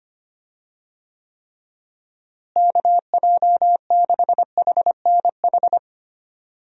Morse code, heard as a keyed tone, about 25 wpm, 700 Hz. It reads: KJ6HNH
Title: USA Call Signs - 6 Characters - Repeated 3x - 25wpm